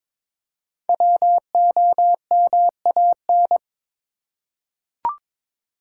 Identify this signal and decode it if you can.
Morse code, heard as a keyed tone, about 22 wpm, 690 Hz. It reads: WOMAN
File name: 5 Letter Words - First 3 Letters Given - 22wpm